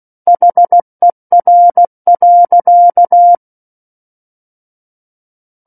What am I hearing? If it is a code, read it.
Morse code, heard as a keyed tone, about 16 wpm, 700 Hz. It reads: HER.
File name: Book One / 07-Chapter 7 - 16 WPM